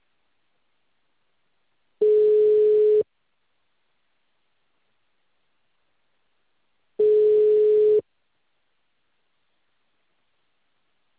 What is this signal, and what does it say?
Ringback tone (Europe, 425 Hz).
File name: out- -2044- - - .2095